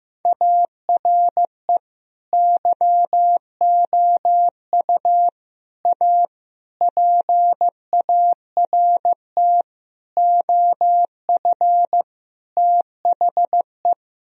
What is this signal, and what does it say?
Morse code, heard as a keyed tone, about 15 wpm, 700 Hz. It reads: ARE YOU A PART OF THE
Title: Sentences from Top 1000 Words - Review Part 4 - 15wpm